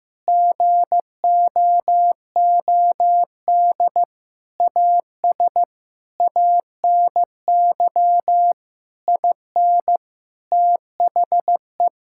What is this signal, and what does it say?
Morse code, heard as a keyed tone, about 16 wpm, 700 Hz. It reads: GOOD AS ANY IN THE
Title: Sentences from Top 1600 Words 15wpm